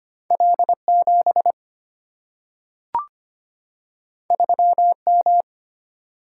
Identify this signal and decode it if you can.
Morse code, heard as a keyed tone, about 25 wpm, 700 Hz. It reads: L7 3M
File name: Two Letter-Number Combinations - 25wpm